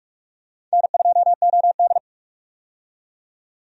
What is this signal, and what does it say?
Morse code, heard as a keyed tone, about 41 wpm, 700 Hz. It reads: N2OD
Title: USA Call Signs - 4 Characters - 45wpm